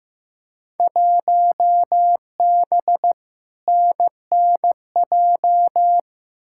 Morse code, heard as a keyed tone, about 15 wpm, 700 Hz. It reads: 1B NNJ
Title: Field Day Exchanges Encore - 15wpm